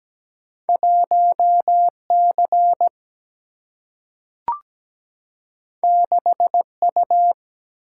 Morse code, heard as a keyed tone, about 17 wpm, 700 Hz. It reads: 1C 6U
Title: Single-Letter Single-Number 17wpm